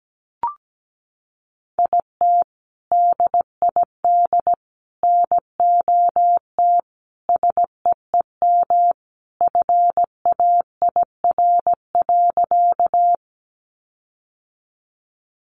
Morse code, heard as a keyed tone, about 17 wpm, 700 Hz. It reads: IT DID NOT SEEM FAIR.
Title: Sentences from Top 1300 Words 17wpm